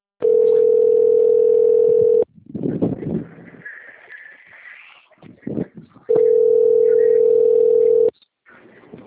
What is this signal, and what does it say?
Ringback tone (North America), 440 + 480 Hz.
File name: rg-1600-unknown- - - .27476